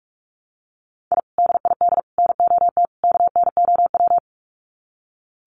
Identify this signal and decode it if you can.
Morse code, heard as a keyed tone, about 44 wpm, 700 Hz. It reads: I DID NOT KNOW
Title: Sets of 4 Words 45wpm